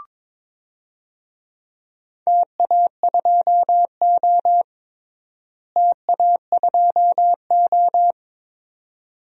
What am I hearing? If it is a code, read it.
Morse code, heard as a keyed tone, about 22 wpm, 700 Hz. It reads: TA2O TA2O
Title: DXCC-100 Call Signs - 4 Characters - Repeated 2x - 22wpm